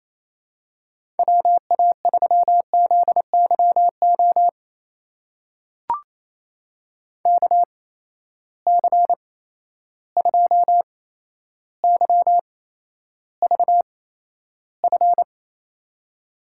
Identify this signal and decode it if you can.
Morse code, heard as a keyed tone, about 27 wpm, spaced as for 12 wpm, 700 Hz. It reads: WA3ZYO KC2YVF
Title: USA Call Signs - 6 Characters - 8x Character Spacing - 28wpm